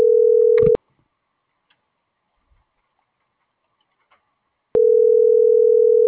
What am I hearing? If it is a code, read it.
Ringback tone (North America), 440 + 480 Hz.